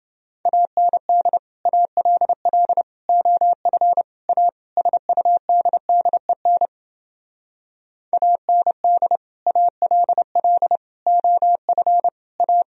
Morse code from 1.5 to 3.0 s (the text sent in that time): ALL